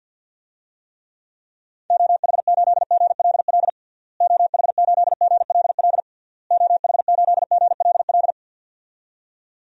Morse code, heard as a keyed tone, about 50 wpm, 700 Hz. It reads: OH8GLL OH8GLL OH8GLL
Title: DXCC-100 Call Signs - 6 Characters - Repeated 3x - 50wpm